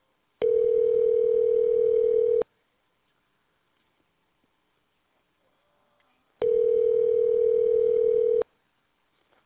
Ringback tone (North America), 440 + 480 Hz.